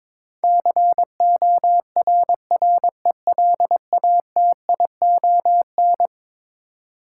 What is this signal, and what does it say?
Morse code, heard as a keyed tone, about 22 wpm, 700 Hz. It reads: CORRELATION
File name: practice / 22wpm / Eleven Letter Words - 22wpm